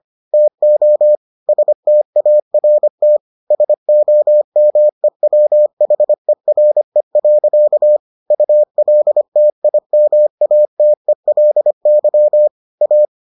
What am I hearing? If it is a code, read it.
Morse code, heard as a keyed tone, about 25 wpm, 600 Hz. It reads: TO START SOMEWHERE. ULTIMATELY A